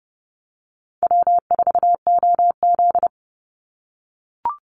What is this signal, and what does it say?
Morse code, heard as a keyed tone, about 30 wpm, 700 Hz. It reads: W4OZ
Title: USA Call Signs 30wpm